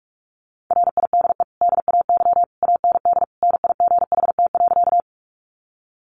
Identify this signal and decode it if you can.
Morse code, heard as a keyed tone, about 44 wpm, 700 Hz. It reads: RIDE DAY AND NIGHT.
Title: Sentences from Top 1200 Words 45wpm